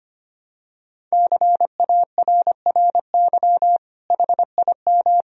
Morse code, heard as a keyed tone, about 25 wpm, 700 Hz. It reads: CARRY HIM
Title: Sentences from Top 1000 Words - Review Part 1 - 25wpm